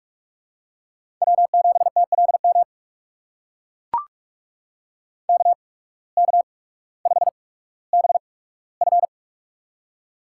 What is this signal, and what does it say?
Morse code, heard as a keyed tone, about 44 wpm, spaced as for 19 wpm, 700 Hz. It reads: W7TLM KK5BF
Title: USA Call Signs - 5 Characters - 8x Character Spacing - 45wpm